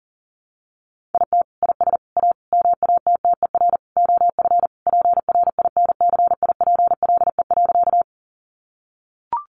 Morse code, heard as a keyed tone, about 40 wpm, 700 Hz. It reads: IT IS A MATTER OF PRINCIPLE.
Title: Sentences from Top 1400 Words 40wpm